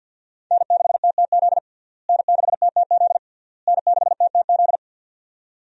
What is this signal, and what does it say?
Morse code, heard as a keyed tone, about 52 wpm, 700 Hz. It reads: N6TTZ N6TTZ N6TTZ
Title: USA Call Signs - 5 Characters - Repeated 3x - 50wpm